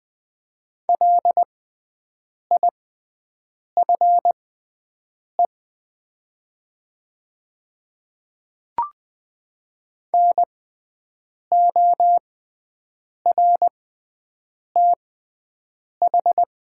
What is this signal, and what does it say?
Morse code, heard as a keyed tone, about 20 wpm, spaced as for 7 wpm, 700 Hz. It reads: LIFE NORTH
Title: 6x Character Spacing - Top 300 Words - 20wpm